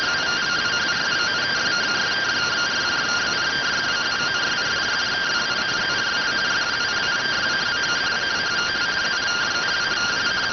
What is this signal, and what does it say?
Two-tone teleprinter signal, about 170 Hz shift.